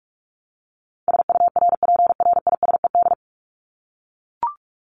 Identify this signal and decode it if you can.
Morse code, heard as a keyed tone, about 44 wpm, 700 Hz. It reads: SURPRISED
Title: Top 1501-1750 Words 45wpm